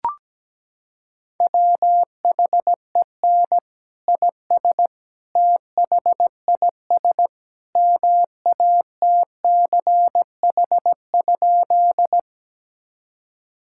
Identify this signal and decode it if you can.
Morse code, heard as a keyed tone, about 17 wpm, 700 Hz. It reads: WHEN IS THIS MATCH?
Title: Sentences from Top 1600 Words 17wpm